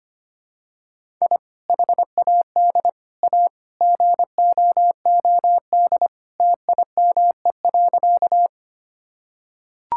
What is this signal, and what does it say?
Morse code, heard as a keyed tone, about 25 wpm, 700 Hz. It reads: I HAD A GOOD TIME.
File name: practice / 25wpm / Sentences with 3-Word Phrases - 25wpm